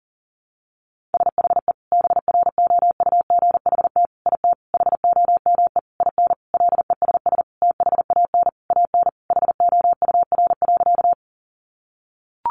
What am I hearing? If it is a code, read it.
Morse code, heard as a keyed tone, about 40 wpm, 700 Hz. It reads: SHE BROUGHT IT HOME IN LESS THAN AN HOUR.
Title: Sentences from Top 500 Words 40wpm